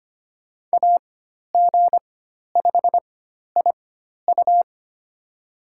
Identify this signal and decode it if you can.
Morse code, heard as a keyed tone, about 25 wpm, spaced as for 12 wpm, 700 Hz. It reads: AG5IU
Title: USA Call Signs - 5 Characters - 4x Character Spacing - 25wpm